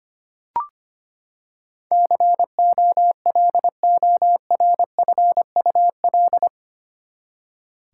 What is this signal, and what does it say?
Morse code, heard as a keyed tone, about 25 wpm, 700 Hz. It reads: COLORFUL
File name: Eight Letter Words - 25wpm